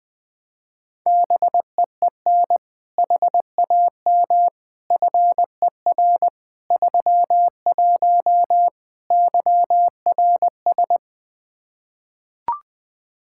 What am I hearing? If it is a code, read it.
Morse code, heard as a keyed tone, about 20 wpm, 700 Hz. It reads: BEEN HAM FER 31 YRS
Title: K1USN SST QSO Practice 20wpm